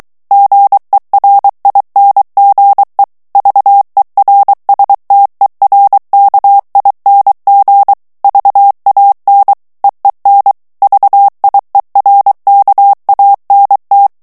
Morse code, heard as a keyed tone, about 23 wpm, 800 Hz. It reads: GERINGE VERSTERKING VAN EEN VIERKANT
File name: HOLLANDAIS texte clair 1 vitesse 22 mots minute